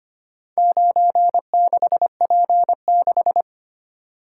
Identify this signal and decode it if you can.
Morse code, heard as a keyed tone, about 25 wpm, 700 Hz. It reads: 96P6
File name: practice / 25wpm / Four Letter-Number Combinations - 25wpm